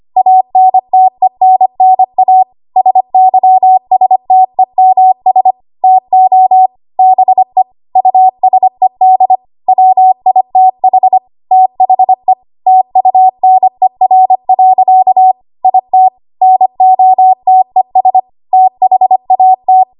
Morse code, heard as a keyed tone, about 25 wpm, 750 Hz. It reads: ANTENNA SYSTEMS TO BE USED WITH THE TUNER. IT NOTES THAT
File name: Archive / 25 WPM / 25WPM